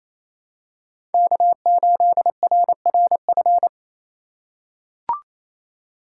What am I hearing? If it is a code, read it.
Morse code, heard as a keyed tone, about 28 wpm, 700 Hz. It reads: K8RRF
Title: USA Call Signs - 5 Characters - 2x Character Spacing - 28wpm